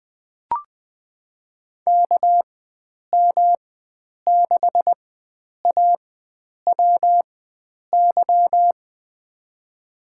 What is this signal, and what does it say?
Morse code, heard as a keyed tone, about 20 wpm, spaced as for 9 wpm, 700 Hz. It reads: KM6AWY
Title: USA Call Signs - 6 Characters - 4x Character Spacing - 20wpm